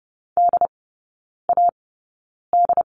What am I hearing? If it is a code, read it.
Morse code, heard as a keyed tone, about 30 wpm, spaced as for 9 wpm, 700 Hz. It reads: DAD
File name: 081 Lesson 14e - Number 2 - Three Characters 30wpm